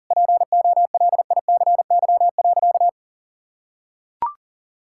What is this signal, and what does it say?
Morse code, heard as a keyed tone, about 40 wpm, 700 Hz. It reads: POLICY.
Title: Sentences from Top 1000 Words - Review Part 6 - 40wpm